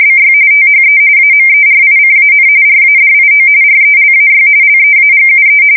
RTTY, about 45.45 baud. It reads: BY THE AMERICAN RADIO RELAY LEAGUE
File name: fsk170 / teletype-RTTY